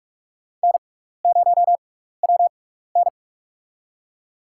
Morse code, heard as a keyed tone, about 46 wpm, spaced as for 15 wpm, 690 Hz. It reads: N0WN
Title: USA Call Signs - 6x Character Spacing - 45wpm